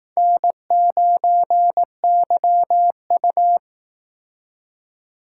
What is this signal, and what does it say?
Morse code, heard as a keyed tone, about 19 wpm, 700 Hz. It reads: N9YU
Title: Speed Racing - USA Call Signs 15wpm